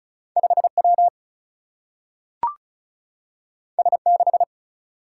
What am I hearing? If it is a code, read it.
Morse code, heard as a keyed tone, about 36 wpm, 700 Hz. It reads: 5W S6
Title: Single-Letter Single-Number 35wpm